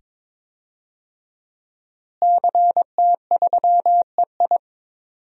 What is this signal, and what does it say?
Morse code, heard as a keyed tone, about 22 wpm, 700 Hz. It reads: CT3EI
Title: DXCC-100 Call Signs - 5 Characters - Repeated 2x - 22wpm